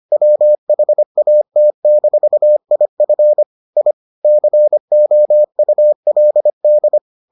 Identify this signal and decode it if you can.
Morse code, heard as a keyed tone, about 25 wpm, 600 Hz. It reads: WHAT-IF I COULD